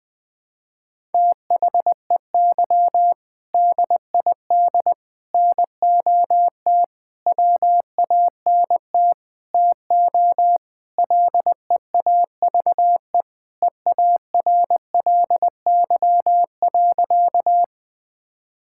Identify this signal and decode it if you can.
Morse code, heard as a keyed tone, about 21 wpm, 700 Hz. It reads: THEY DID NOT WANT TO LEAVE EARLY.
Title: Sentences with 5-Word Phrases - 20wpm